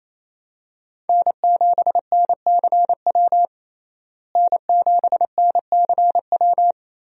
Morse code, heard as a keyed tone, about 28 wpm, 700 Hz. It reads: N7NCW N7NCW
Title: USA Call Signs - 5 Characters - Repeated 2x - 28wpm